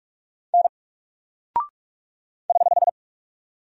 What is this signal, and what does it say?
Morse code, heard as a keyed tone, about 44 wpm, 700 Hz. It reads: N <HH>
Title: Single Character (Letter:Number:Punctuation:Prosign) 45wpm